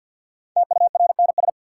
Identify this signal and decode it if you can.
Morse code, heard as a keyed tone, about 52 wpm, 700 Hz. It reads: TURNS
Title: Sets of 5 Words Encore - New Word Drill - 50wpm